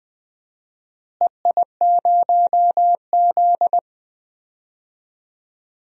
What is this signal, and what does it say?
Morse code, heard as a keyed tone, about 20 wpm, 700 Hz. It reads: EI0Z
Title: DXCC-100 Call Signs - 4 Characters - Repeated 2x - 20wpm